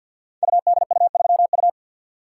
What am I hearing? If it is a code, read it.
Morse code, heard as a keyed tone, about 53 wpm, 700 Hz. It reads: UDU3U
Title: Code Groups - 5 Letter:Numbers - 50wpm